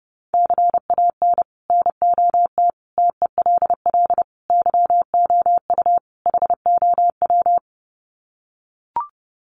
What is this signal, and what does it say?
Morse code, heard as a keyed tone, about 30 wpm, 700 Hz. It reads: CAN NOT TELL YOU HOW
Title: Sets of 5 Words Encore2 30wpm